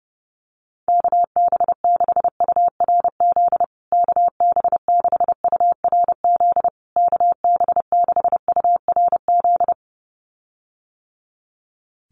Morse code, heard as a keyed tone, about 30 wpm, 700 Hz. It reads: KB6URZ KB6URZ KB6URZ
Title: USA Call Signs Repeated 3x 30wpm